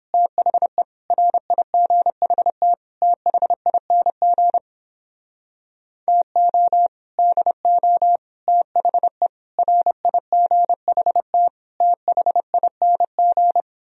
Morse code, heard as a keyed tone, about 29 wpm, 700 Hz. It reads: THE RIGHT THING TO DO THE RIGHT THING